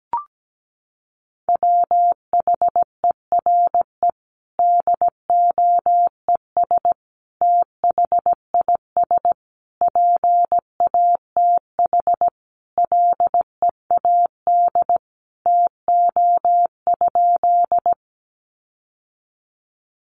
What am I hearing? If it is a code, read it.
Morse code, heard as a keyed tone, about 17 wpm, 700 Hz. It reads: WHERE DOES THIS PATH LEAD TO?
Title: Sentences from Top 1400 Words 17wpm